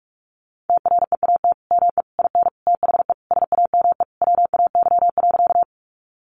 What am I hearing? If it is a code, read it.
Morse code, heard as a keyed tone, about 46 wpm, 700 Hz. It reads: TREAT ME IN THE SAME WAY.
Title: Sentences from Top 1300 Words 45wpm